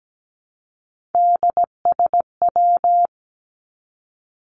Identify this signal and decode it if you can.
Morse code, heard as a keyed tone, about 17 wpm, 700 Hz. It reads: DSW
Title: Abbreviations 17wpm